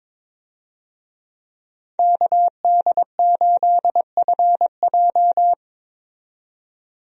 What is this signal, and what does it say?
Morse code, heard as a keyed tone, about 21 wpm, 700 Hz. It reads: KD8FJ
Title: USA Call Signs - 5 Characters - 22wpm